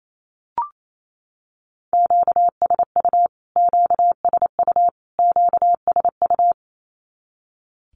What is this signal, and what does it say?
Morse code, heard as a keyed tone, about 28 wpm, 700 Hz. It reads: QSU QSU QSU
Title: Q-Codes Repeated 3x 28wpm